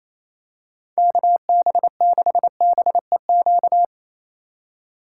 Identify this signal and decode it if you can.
Morse code, heard as a keyed tone, about 28 wpm, 700 Hz. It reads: KB6BEQ
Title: USA Call Signs - 6 Characters - 3x Character Spacing - 28wpm